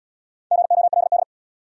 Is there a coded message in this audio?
Yes: Morse code from 0.5 to 1.2 s, heard as a keyed tone, about 74 wpm, 700 Hz. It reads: GOLD